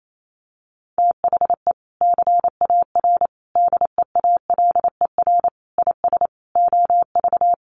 Morse code, heard as a keyed tone, about 28 wpm, 700 Hz. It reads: THE CAR DEALER IS OV
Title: Sentences from Top 1100 Words 28wpm